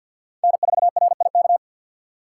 Morse code, heard as a keyed tone, about 51 wpm, 700 Hz. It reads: N4RIK
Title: USA Call Signs - 5 Characters - 6x Character Spacing - 50wpm